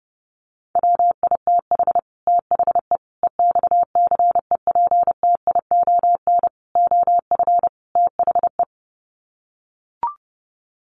Morse code, heard as a keyed tone, about 30 wpm, 700 Hz. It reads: WITH THE EXCEPTION OF THE
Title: Sets of 5 Words Encore 30wpm